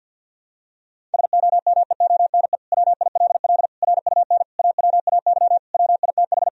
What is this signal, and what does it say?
Morse code, heard as a keyed tone, about 48 wpm, 700 Hz. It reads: SOMEONE WILL RUN AWAY WITH
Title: Sentences from Top 1000 Words - Review Part 1 - 50wpm